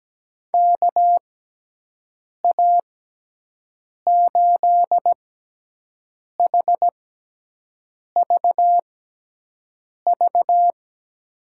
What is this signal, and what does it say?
Morse code, heard as a keyed tone, about 17 wpm, spaced as for 6 wpm, 700 Hz. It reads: KA8HVV
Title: USA Call Signs - 6 Characters - 6x Character Spacing - 17wpm